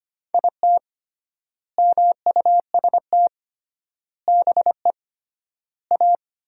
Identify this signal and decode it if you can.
Morse code, heard as a keyed tone, about 25 wpm, 700 Hz. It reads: IT MUST BE A
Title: 3x Word-Spacing - Sentences from Top 500 Words - 25wpm